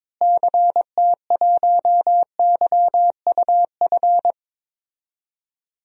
Morse code, heard as a keyed tone, about 21 wpm, 700 Hz. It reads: CT1YUF
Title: DXCC-100 Call Signs - 6 Characters - Repeated 3x - 22wpm